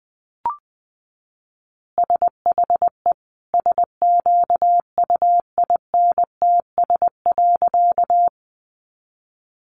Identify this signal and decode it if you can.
Morse code, heard as a keyed tone, about 20 wpm, 700 Hz. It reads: SHE SQUINTS.
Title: Additional Easy Phrases 20wpm